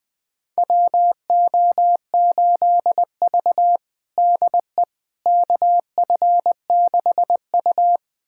Morse code, heard as a keyed tone, about 20 wpm, 700 Hz. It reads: WO8V DE KF6U
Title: USA Callsign DE Callsign - 4 Characters - 20wpm